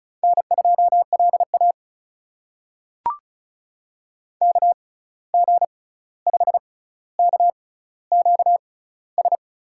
Morse code, heard as a keyed tone, about 36 wpm, spaced as for 16 wpm, 700 Hz. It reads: N2LA KG5KQS
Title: USA Call Signs - 6x Character Spacing - 35wpm